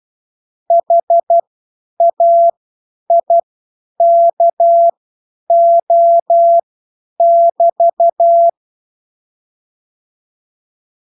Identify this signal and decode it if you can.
Morse code, heard as a keyed tone, about 12 wpm, spaced as for 9 wpm, 680 Hz. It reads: HAIKO<BT>